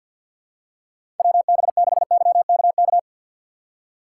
Morse code, heard as a keyed tone, about 53 wpm, 700 Hz. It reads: WB6YKK
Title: USA Call Signs - 6 Characters - 6x Character Spacing - 50wpm